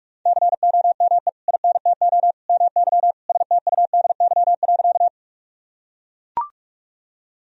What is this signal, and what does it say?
Morse code, heard as a keyed tone, about 41 wpm, 700 Hz. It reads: COME INTO MY STUDY.
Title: Sentences from Top 1000 Words - Review Part 1 - 45wpm